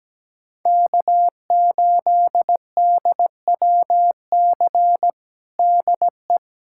Morse code, heard as a keyed tone, about 17 wpm, 700 Hz. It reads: K8DWC DE
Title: USA Callsign DE Callsign - 5 Characters - 17wpm